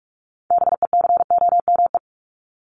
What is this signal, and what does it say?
Morse code, heard as a keyed tone, about 44 wpm, 700 Hz. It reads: BECOME
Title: ING Suffix - Top 100 Words with Base Words 45wpm